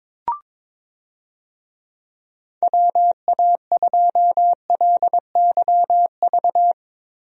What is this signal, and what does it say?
Morse code, heard as a keyed tone, about 21 wpm, 700 Hz. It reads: WA2LYV
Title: USA Call Signs - 6 Characters - 22wpm